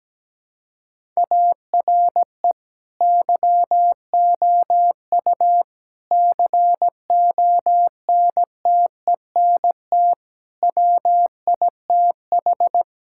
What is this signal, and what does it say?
Morse code, heard as a keyed tone, about 17 wpm, 700 Hz. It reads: ARE YOU CONTENT WITH
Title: Sentences from Top 1000 Words - Review Part 6 - 17wpm